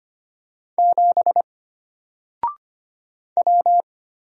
Morse code, heard as a keyed tone, about 25 wpm, 700 Hz. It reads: 7 W
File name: Single-Letter-Number-25wpm